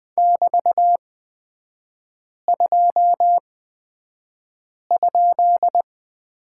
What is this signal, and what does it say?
Morse code, heard as a keyed tone, about 20 wpm, spaced as for 5 wpm, 700 Hz. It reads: <BT>2?
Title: 224 Lesson 36e - BT - Three Characters 20wpm